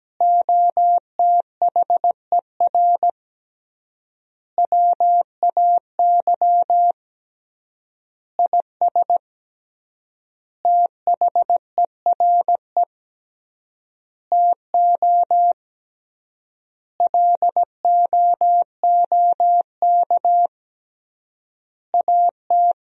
Morse code, heard as a keyed tone, about 17 wpm, 700 Hz. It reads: OTHER WAY IS THERE TO LOOK AT
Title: 3x Word-Spacing - Sentences from Top 100 Words - 17wpm